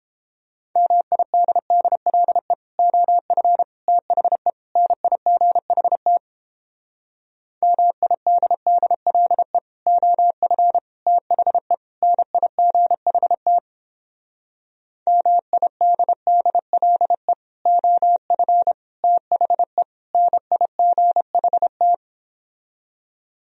Morse code, heard as a keyed tone, about 29 wpm, 700 Hz. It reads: MIDDLE OF THE NIGHT MIDDLE OF THE NIGHT MIDDLE OF THE NIGHT